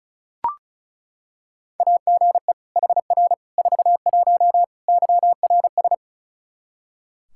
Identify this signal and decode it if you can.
Morse code, heard as a keyed tone, about 34 wpm, 700 Hz. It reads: AGE HR 41 YRS
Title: Practice QSO - Exchange 3 - More Info - 35wpm